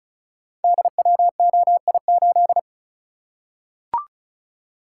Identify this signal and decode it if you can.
Morse code, heard as a keyed tone, about 36 wpm, 700 Hz. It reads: DWOI8
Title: Code Groups - 5 Letter:Numbers - 35wpm